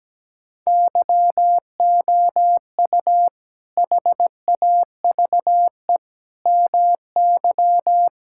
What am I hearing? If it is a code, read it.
Morse code, heard as a keyed tone, about 17 wpm, 700 Hz. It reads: YOU HAVE MY
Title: Sentences from Top 1000 Words - Review Part 6 - 17wpm